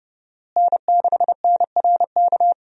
Morse code, heard as a keyed tone, about 30 wpm, 700 Hz. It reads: N6NRK
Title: USA Call Signs - 5 Characters - Repeated 2x - 30wpm